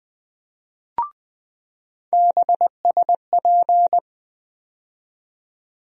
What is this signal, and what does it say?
Morse code, heard as a keyed tone, about 20 wpm, 700 Hz. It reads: BSP
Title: USA Call Sign Postfixes - 3 Letters 20wpm